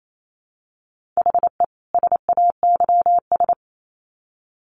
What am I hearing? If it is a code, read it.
Morse code, heard as a keyed tone, about 28 wpm, 700 Hz. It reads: HE SAYS
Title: Sets of 2 Words 28wpm